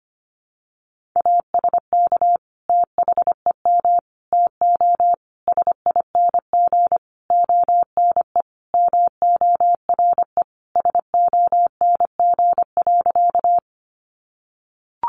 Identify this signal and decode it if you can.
Morse code, heard as a keyed tone, about 25 wpm, 700 Hz. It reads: ASK THEM TO SING ONE MORE SONG.
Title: Sentences from Top 400 Words 25wpm